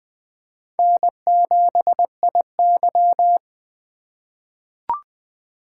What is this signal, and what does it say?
Morse code, heard as a keyed tone, about 21 wpm, 700 Hz. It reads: N7IY
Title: USA Call Signs - 4 Characters - 2x Character Spacing - 20wpm